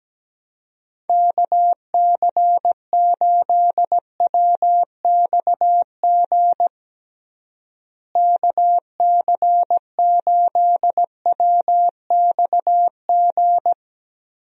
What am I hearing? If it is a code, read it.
Morse code, heard as a keyed tone, about 17 wpm, 700 Hz. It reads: KC8WXG KC8WXG
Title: USA Call Signs - 6 Characters - Repeated 2x - 17wpm